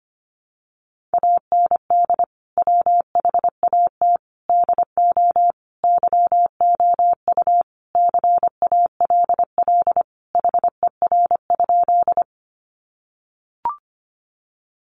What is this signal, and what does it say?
Morse code, heard as a keyed tone, about 25 wpm, 700 Hz. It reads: AND WHAT DO YOU CALL HER?
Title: Sentences from Top 100 Words 25wpm